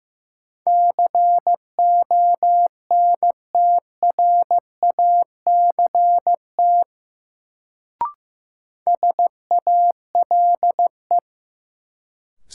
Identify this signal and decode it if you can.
Morse code, heard as a keyed tone, about 15 wpm, 700 Hz. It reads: CONTRACT SALE
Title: Sentences from Top 1400 Words - New Word Drill - 15wpm